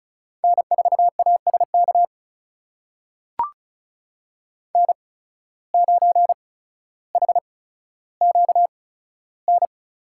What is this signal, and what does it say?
Morse code, heard as a keyed tone, about 34 wpm, spaced as for 15 wpm, 700 Hz. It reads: N4ASK N9HQN
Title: USA Call Signs - 5 Characters - 8x Character Spacing - 35wpm